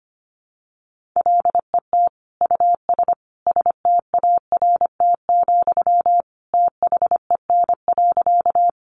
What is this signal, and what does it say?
Morse code, heard as a keyed tone, about 25 wpm, 700 Hz. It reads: LET US START, THEN.
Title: Sentences from Top 300 Words 25wpm